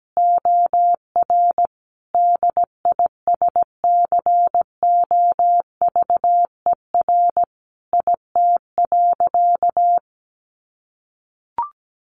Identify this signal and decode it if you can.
Morse code, heard as a keyed tone, about 17 wpm, 700 Hz. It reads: OR DISCOVER IT.
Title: Sentences from Top 1100 Words 17wpm